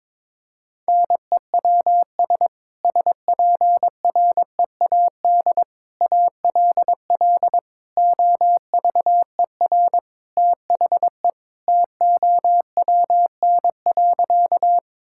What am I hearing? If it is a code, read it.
Morse code, heard as a keyed tone, about 22 wpm, 700 Hz. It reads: NEWS SPREAD ALL OVER THE TOWN.
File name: practice / 22wpm / Sentences with 3-Word Phrases - 22wpm